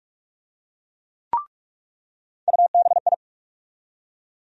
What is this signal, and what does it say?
Morse code, heard as a keyed tone, about 48 wpm, 700 Hz. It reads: UBI
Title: USA Call Sign Postfixes - 3 Letters 45wpm